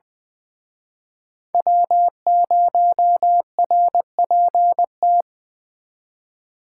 Morse code, heard as a keyed tone, about 23 wpm, 700 Hz. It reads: W0RPT
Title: Speed Racing - USA 1x3 Call Signs 20wpm